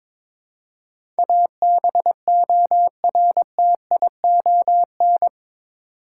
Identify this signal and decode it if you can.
Morse code, heard as a keyed tone, about 22 wpm, 700 Hz. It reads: ABORTION
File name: practice / 22wpm / Eight Letter Words - 22wpm